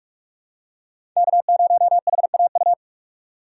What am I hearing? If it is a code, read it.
Morse code, heard as a keyed tone, about 46 wpm, 700 Hz. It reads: K0HAU
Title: USA Call Signs - 5 Characters - 2x Character Spacing - 45wpm